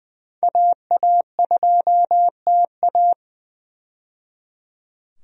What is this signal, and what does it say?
Morse code, heard as a keyed tone, about 20 wpm, 700 Hz. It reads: AA2TA
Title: USA Call Signs - 5 Characters - 20wpm